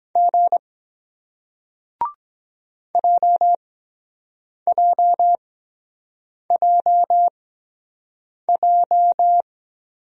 Morse code, heard as a keyed tone, about 23 wpm, spaced as for 7 wpm, 700 Hz. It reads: G JJJJ